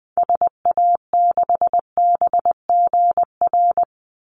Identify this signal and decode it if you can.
Morse code, heard as a keyed tone, about 19 wpm, 700 Hz. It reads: SA6BGR
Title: Call Signs - 20wpm